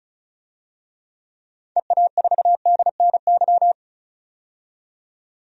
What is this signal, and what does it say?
Morse code, heard as a keyed tone, about 36 wpm, 700 Hz. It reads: EA4DNY
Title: DXCC-100 Call Signs - 6 Characters - 35wpm